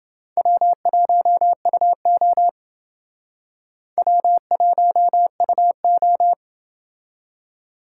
Morse code, heard as a keyed tone, about 28 wpm, 700 Hz. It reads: W1UO W1UO